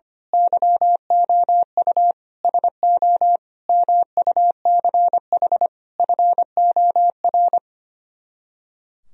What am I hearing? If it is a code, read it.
Morse code, heard as a keyed tone, about 24 wpm, 700 Hz. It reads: YOU SO MUCH FOR
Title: Speed-Racing - Sets of 5 Words 25wpm